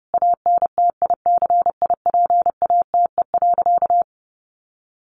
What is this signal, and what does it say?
Morse code, heard as a keyed tone, about 30 wpm, 700 Hz. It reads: ANTICIPATE.
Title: Sentences from Top 1500 Words 30wpm